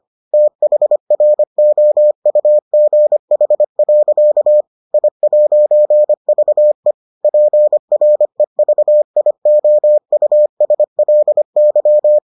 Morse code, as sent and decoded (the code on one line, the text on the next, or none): - .... .-. --- ..- --. .... .-.-.- / .. .----. ...- . / .--. .-. ...- .. --- ..- ... .-.. -.--
THROUGH. I'VE PREVIOUSLY